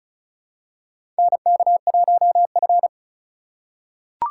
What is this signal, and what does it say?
Morse code, heard as a keyed tone, about 36 wpm, 690 Hz. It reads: NK1F
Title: USA Call Signs - 3x Character Spacing - 35wpm